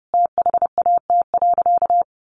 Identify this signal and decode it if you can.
Morse code, heard as a keyed tone, about 30 wpm, 700 Hz. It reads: THAT.
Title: Sentences from Top 1000 Words 30wpm